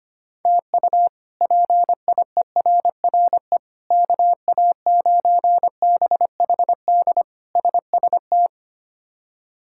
Morse code, heard as a keyed tone, about 25 wpm, 700 Hz. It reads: TU PIERRE KA9BHD SST
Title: K1USN SST QSO Practice - 25wpm